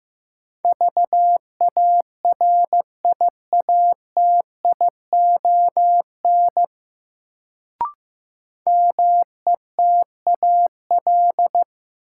Morse code, heard as a keyed tone, about 15 wpm, 700 Hz. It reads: VARIATION METAL
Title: Sentences from Top 1600 Words - New Word Drill Alt - 15wpm